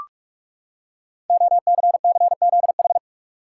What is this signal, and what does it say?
Morse code, heard as a keyed tone, about 44 wpm, 690 Hz. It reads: OCCZH
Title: Code Groups - 5 Letters - 45wpm